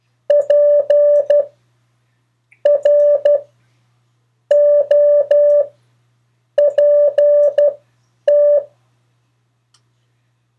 Morse code, heard as a keyed tone, about 12 wpm, spaced as for 6 wpm, 580 Hz. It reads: PROPT
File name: 4of4-sentences-mistakes-test